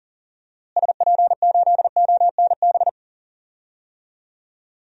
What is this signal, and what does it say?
Morse code, heard as a keyed tone, about 40 wpm, 700 Hz. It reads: SP8ONB
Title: DXCC-100 Call Signs - 6 Characters - Repeated 3x - 40wpm